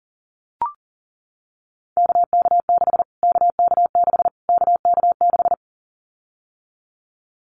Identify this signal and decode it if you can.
Morse code, heard as a keyed tone, about 40 wpm, 700 Hz. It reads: KK6 KK6 KK6
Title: Callsign Prefixes for Top 25 DXCC - Repeated 3x - 40wpm